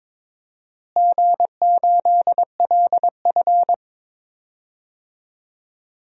Morse code, heard as a keyed tone, about 22 wpm, 700 Hz. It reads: G8LF
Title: DXCC-100 Call Signs - 4 Characters - Repeated 3x - 22wpm